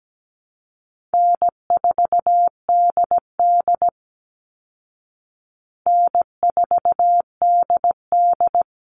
Morse code, heard as a keyed tone, about 17 wpm, 700 Hz. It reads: N4DD N4DD
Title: Call Signs Repeated 3x - 17wpm